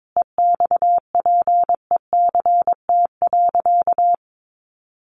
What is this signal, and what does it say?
Morse code, heard as a keyed tone, about 22 wpm, 700 Hz. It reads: EXPECT.
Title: Sentences from Top 1200 Words 22wpm